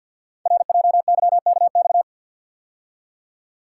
Morse code, heard as a keyed tone, about 48 wpm, 690 Hz. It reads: RJYKX
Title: Code Groups - 5 Letters - 50wpm